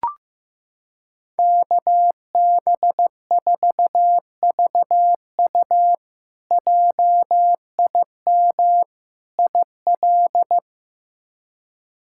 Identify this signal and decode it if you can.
Morse code, heard as a keyed tone, about 16 wpm, 700 Hz. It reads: KB4VU JIM IL
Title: K1USN SST QSO Practice - 15wpm